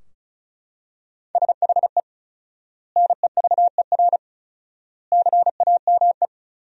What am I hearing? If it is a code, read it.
Morse code, heard as a keyed tone, about 34 wpm, 700 Hz. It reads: SHE NEVER CAME
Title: 4x Word-Spacing - Sentences from Top 300 Words - 35wpm